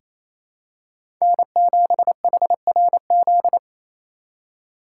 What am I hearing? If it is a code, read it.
Morse code, heard as a keyed tone, about 28 wpm, 700 Hz. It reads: N7HRZ
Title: USA Call Signs - 2x Character Spacing - 28wpm